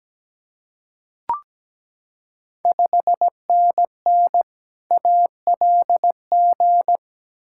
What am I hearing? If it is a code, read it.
Morse code, heard as a keyed tone, about 17 wpm, 690 Hz. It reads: 5NN ALG
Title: Ontario QSO Party Exchange 17wpm